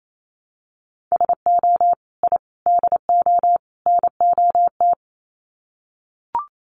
Morse code, heard as a keyed tone, about 28 wpm, 700 Hz. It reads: SO I DO NOT
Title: Sets of 4 Words 28wpm